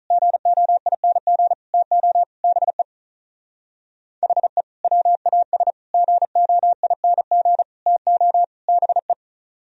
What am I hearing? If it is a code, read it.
Morse code, heard as a keyed tone, about 43 wpm, 700 Hz. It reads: GOING TO BE HE WAS GOING TO BE
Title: Speed-Racing - Sets of 5 Words 35wpm